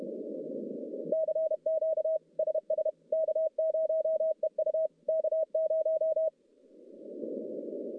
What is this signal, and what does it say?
Morse code, heard as a keyed tone, about 31 wpm, 640 Hz. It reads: CQ SS K0EU K0
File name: contests / CW-SS-k5tr- -1200